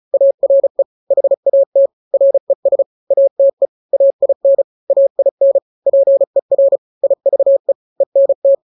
Morse code, heard as a keyed tone, about 34 wpm, 550 Hz. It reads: ARE HAT RES ATE AIN AIN PER IVE ENT
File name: 3-grams 35 30 WPM